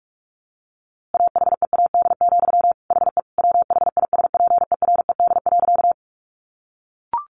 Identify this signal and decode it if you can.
Morse code, heard as a keyed tone, about 44 wpm, 700 Hz. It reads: AHEAD, HE WHISPERED.
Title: A Sound Of Thunder 45wpm